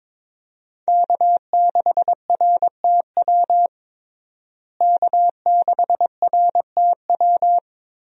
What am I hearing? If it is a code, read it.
Morse code, heard as a keyed tone, about 22 wpm, 700 Hz. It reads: K6RTW K6RTW
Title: USA Call Signs - 5 Characters - Repeated 2x - 22wpm